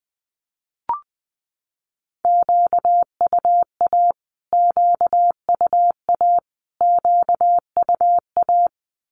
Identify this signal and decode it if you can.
Morse code, heard as a keyed tone, about 20 wpm, 700 Hz. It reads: QUA QUA QUA
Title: Q-Codes Repeated 3x 20wpm